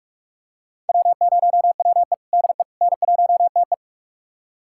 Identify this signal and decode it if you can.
Morse code, heard as a keyed tone, about 44 wpm, 690 Hz. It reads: W0WE DE N1TE
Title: USA Callsign DE Callsign - 4 Characters - 45wpm